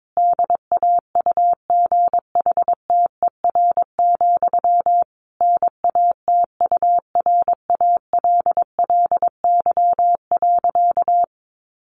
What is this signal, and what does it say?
Morse code, heard as a keyed tone, about 22 wpm, 700 Hz. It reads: DAUGHTER, NATURALLY.
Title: Sentences from Top 1500 Words 22wpm